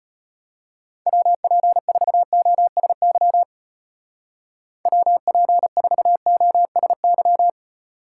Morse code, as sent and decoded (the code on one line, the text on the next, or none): .-- .--. ....- --- ... -.-- / .-- .--. ....- --- ... -.--
WP4OSY WP4OSY